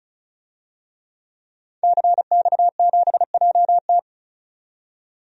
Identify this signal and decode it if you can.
Morse code, heard as a keyed tone, about 35 wpm, 700 Hz. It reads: CX7JT
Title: DXCC-100 Call Signs - 5 Characters - 35wpm